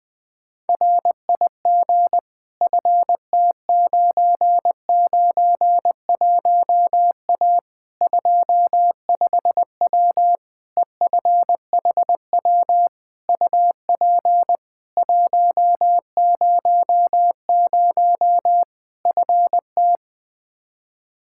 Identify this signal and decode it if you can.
Morse code, heard as a keyed tone, about 20 wpm, 700 Hz. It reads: RIG FT991A 25W EFHW UP 100 FT
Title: Practice QSO - Exchange 2 - Info - 20wpm